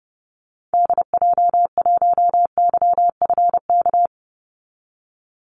Morse code, heard as a keyed tone, about 30 wpm, 700 Hz. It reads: DJ1YFK
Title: Call Signs Repeated 2x - 30wpm